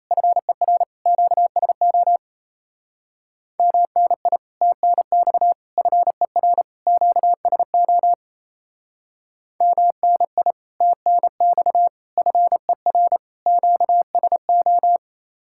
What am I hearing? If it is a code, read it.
Morse code, heard as a keyed tone, about 35 wpm, 700 Hz. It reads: FER QSO MNI TNX FER QSO MNI TNX FER QSO
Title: Speed-Racing Common QSO Phrases - 28wpm